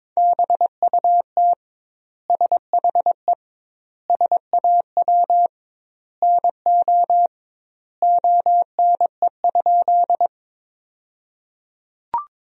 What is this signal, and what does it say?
Morse code, heard as a keyed tone, about 22 wpm, 700 Hz. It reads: BUT SHE SAW NO ONE?
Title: 2x Word-Spacing - Sentences from Top 300 Words - 22wpm